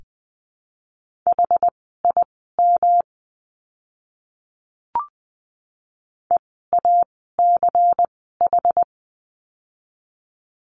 Morse code, heard as a keyed tone, about 20 wpm, spaced as for 14 wpm, 700 Hz. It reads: HIM EACH
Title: 230 Lesson 37f - Letter x - Words 20wpm